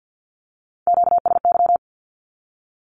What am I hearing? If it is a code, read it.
Morse code, heard as a keyed tone, about 50 wpm, 700 Hz. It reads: QSY
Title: Q-Codes with Spoken Meaning - 50wpm